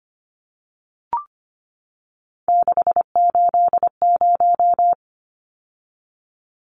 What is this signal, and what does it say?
Morse code, heard as a keyed tone, about 25 wpm, 700 Hz. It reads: 680
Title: Numbers - 3-Digit 25wpm